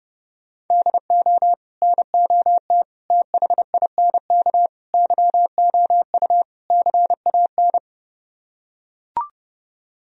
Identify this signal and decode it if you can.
Morse code, heard as a keyed tone, about 30 wpm, 700 Hz. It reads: DO NOT THINK YOU CAN